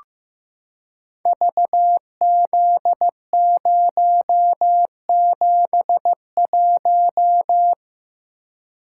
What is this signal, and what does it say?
Morse code, heard as a keyed tone, about 15 wpm, 700 Hz. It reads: VZ071
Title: Code Groups - 5 Letter:Numbers - 15wpm